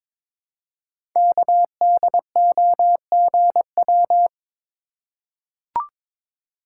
Morse code, heard as a keyed tone, about 22 wpm, 700 Hz. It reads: KDOGW
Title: Code Groups - 5 Letters - 22wpm